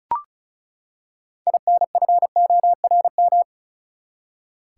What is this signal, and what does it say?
Morse code, heard as a keyed tone, about 36 wpm, 700 Hz. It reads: INFORM
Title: Six Letter Words - 35wpm